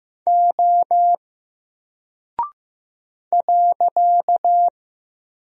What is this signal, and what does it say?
Morse code, heard as a keyed tone, about 15 wpm, 700 Hz. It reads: O .
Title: Single Character (Letter:Number:Punctuation:Prosign) Alt 15wpm